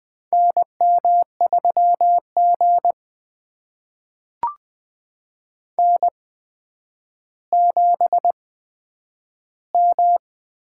Morse code, heard as a keyed tone, about 20 wpm, spaced as for 10 wpm, 700 Hz. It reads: NM3G N7M
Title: USA Call Signs - 4 Characters - 8x Character Spacing - 20wpm